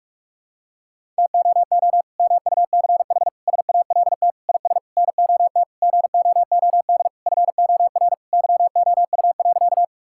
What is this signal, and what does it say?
Morse code, heard as a keyed tone, about 46 wpm, 700 Hz. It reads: TOO MUCH SALT IS NOT GOOD FOR YOU.
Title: Sentences from Top 1000 Words - Review Part 5 - 45wpm